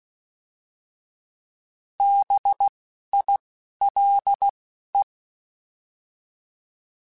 Morse code, heard as a keyed tone, about 16 wpm, 800 Hz. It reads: B I L E